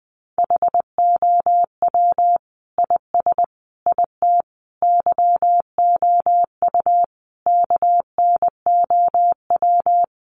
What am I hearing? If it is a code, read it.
Morse code, heard as a keyed tone, about 20 wpm, 700 Hz. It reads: HOW IS IT YOU KNOW
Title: Sentences from Top 700 Words 20wpm